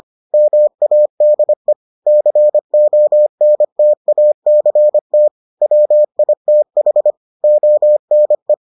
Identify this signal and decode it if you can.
Morse code, heard as a keyed tone, about 25 wpm, 600 Hz. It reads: MADE CONTACT WITH ONE